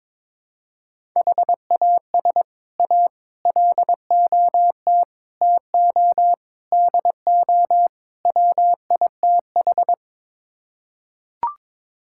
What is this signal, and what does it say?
Morse code, heard as a keyed tone, about 22 wpm, 700 Hz. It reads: HAS A LOT TO DO WITH
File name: practice / 22wpm / Sets of 6 Words - 22wpm